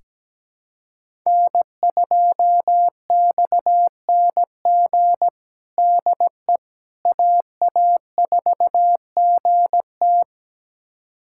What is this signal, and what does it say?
Morse code, heard as a keyed tone, about 17 wpm, 700 Hz. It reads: N2XNG DE AA4GT
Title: USA Callsign DE Callsign - 5 Characters - 17wpm